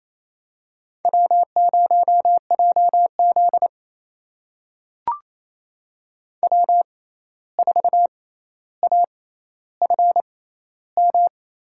Morse code, heard as a keyed tone, about 28 wpm, spaced as for 13 wpm, 700 Hz. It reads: W0JZ W4AFM
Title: USA Call Signs - 6x Character Spacing - 28wpm